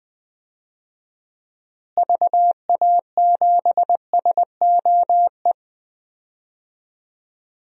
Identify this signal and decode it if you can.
Morse code, heard as a keyed tone, about 20 wpm, 700 Hz. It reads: VA7SOE
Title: DXCC-100 Call Signs - 6 Characters - Repeated 2x - 20wpm